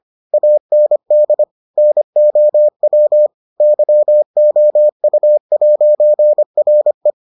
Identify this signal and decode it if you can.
Morse code, heard as a keyed tone, about 25 wpm, 600 Hz. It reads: AND NOW YOU'RE